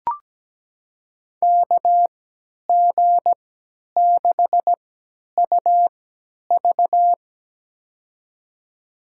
Morse code, heard as a keyed tone, about 17 wpm, spaced as for 10 wpm, 700 Hz. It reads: KG6UV
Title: USA Call Signs - 5 Characters - 3x Character Spacing - 17wpm